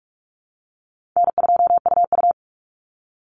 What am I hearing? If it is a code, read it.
Morse code, heard as a keyed tone, about 44 wpm, 700 Hz. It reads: N2UU
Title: Easy Call Signs Repeated 3x - 45wpm